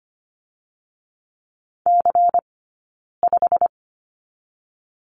Morse code, heard as a keyed tone, about 25 wpm, 700 Hz. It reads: C 5
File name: Lesson 17d - Letter u - Two Characters 25wpm